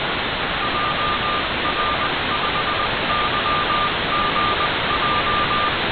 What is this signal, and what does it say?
Morse code, heard as a keyed tone, about 20 wpm, spaced as for 18 wpm, 1200 Hz, with noise. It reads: JR5YGJ
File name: Kosen-1 CW Beacon received by Igor PU4ELT